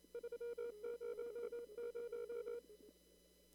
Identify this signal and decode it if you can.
Morse code, heard as a keyed tone, about 28 wpm, 470 Hz. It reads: 300
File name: CW / kt5j wpx cw-op-k5pi